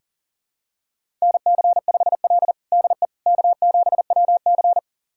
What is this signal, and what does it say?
Morse code, heard as a keyed tone, about 37 wpm, 700 Hz. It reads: NC5L DE K7WC
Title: USA Callsign DE Callsign - 4 Characters - 40wpm